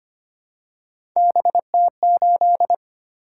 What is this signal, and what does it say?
Morse code, heard as a keyed tone, about 25 wpm, 700 Hz. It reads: BT8
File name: Three Letter-Number Combinations - 25wpm